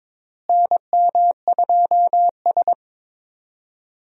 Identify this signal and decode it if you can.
Morse code, heard as a keyed tone, about 21 wpm, 700 Hz. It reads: NM2S
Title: USA Call Signs - 4 Characters - Repeated 3x - 22wpm